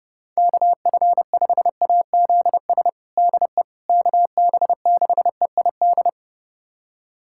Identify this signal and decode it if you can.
Morse code, heard as a keyed tone, about 32 wpm, 700 Hz. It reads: KF5AZS DE KB6EID
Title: USA Callsign DE Callsign - 6 Characters - 30wpm